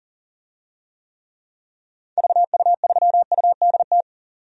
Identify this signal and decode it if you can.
Morse code, heard as a keyed tone, about 40 wpm, 700 Hz. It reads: VU3UDT
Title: DXCC-100 Call Signs - 6 Characters - 40wpm